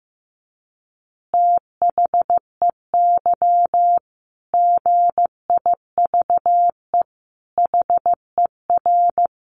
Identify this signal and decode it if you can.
Morse code, heard as a keyed tone, about 15 wpm, 700 Hz. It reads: THEY GIVE HER
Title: Easy Phrases 15wpm